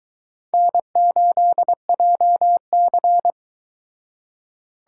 Morse code, heard as a keyed tone, about 22 wpm, 700 Hz. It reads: N8JC